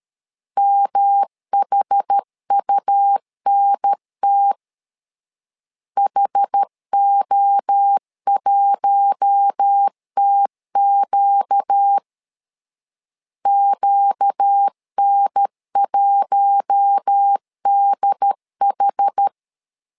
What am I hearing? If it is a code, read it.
Morse code, heard as a keyed tone, about 13 wpm, 800 Hz. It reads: MHUNT HO1TQ QN1DH
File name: LEZIONI / 4 Q1 12wpm 2028